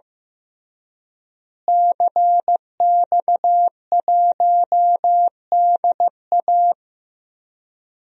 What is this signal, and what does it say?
Morse code, heard as a keyed tone, about 15 wpm, 700 Hz. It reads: CX1DA
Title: DXCC-100 Call Signs - 5 Characters - Repeated 3x - 15wpm